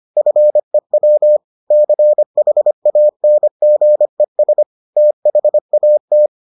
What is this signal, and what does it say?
Morse code, heard as a keyed tone, about 25 wpm, 600 Hz. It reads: FEW CHANGES THAT